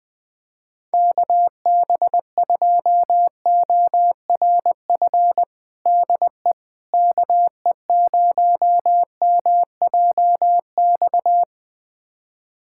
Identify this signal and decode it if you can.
Morse code, heard as a keyed tone, about 20 wpm, 700 Hz. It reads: KB2ORF DE KE0MJX
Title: USA Callsign DE Callsign - 6 Characters - 20wpm